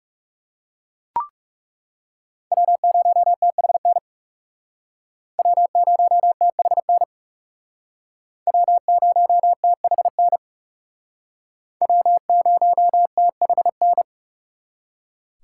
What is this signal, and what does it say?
Morse code, heard as a keyed tone, about 39 wpm, 700 Hz. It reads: W0THN W0THN W0THN W0THN